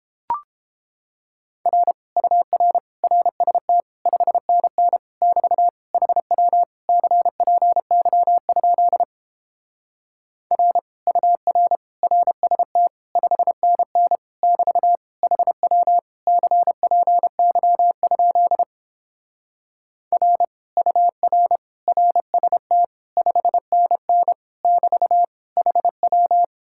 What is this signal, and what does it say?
Morse code, heard as a keyed tone, about 31 wpm, 700 Hz. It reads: R UR RST 5NN <BT> HW CPY? R UR RST 5NN <BT> HW CPY? R UR RST 5NN <BT> HW